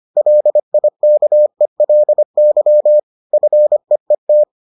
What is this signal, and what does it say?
Morse code, heard as a keyed tone, about 25 wpm, 600 Hz. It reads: LIKELY FEET